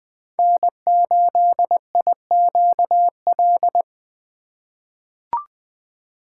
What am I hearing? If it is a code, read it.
Morse code, heard as a keyed tone, about 20 wpm, 700 Hz. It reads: N8IQL
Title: USA Call Signs - 5 Characters - 4x Character Spacing - 20wpm